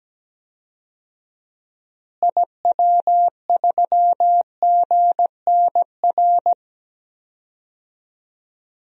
Morse code, heard as a keyed tone, about 17 wpm, 700 Hz. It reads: IW3GNR
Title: DXCC-100 Call Signs - 6 Characters - Repeated 2x - 17wpm